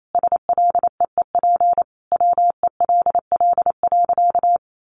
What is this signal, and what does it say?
Morse code, heard as a keyed tone, about 28 wpm, 700 Hz. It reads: SLEEP WELL.
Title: Sentences from Top 500 Words 28wpm